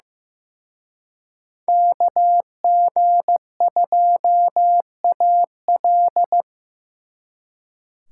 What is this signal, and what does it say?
Morse code, heard as a keyed tone, about 18 wpm, 700 Hz. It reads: KG2AL
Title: Speed Racing - USA 2x2 Call Signs 15wpm